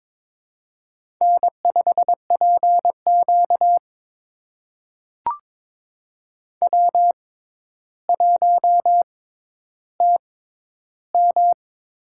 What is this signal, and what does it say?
Morse code, heard as a keyed tone, about 22 wpm, spaced as for 11 wpm, 700 Hz. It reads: N5PQ W1TM
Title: USA Call Signs - 4 Characters - 6x Character Spacing - 22wpm